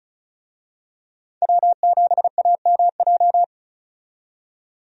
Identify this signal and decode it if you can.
Morse code, heard as a keyed tone, about 34 wpm, 700 Hz. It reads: W7AMJ
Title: USA Call Signs - 5 Characters - 35wpm